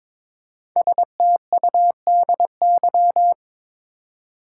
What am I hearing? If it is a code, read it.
Morse code, heard as a keyed tone, about 22 wpm, 690 Hz. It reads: STUDY
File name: 5 Letter Words - First 3 Letters Given - 22wpm